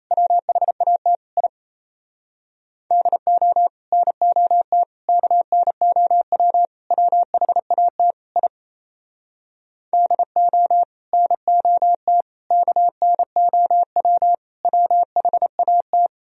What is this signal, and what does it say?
Morse code, heard as a keyed tone, about 38 wpm, 700 Hz. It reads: WHAT I DO NOT KNOW WHAT I DO NOT KNOW WHAT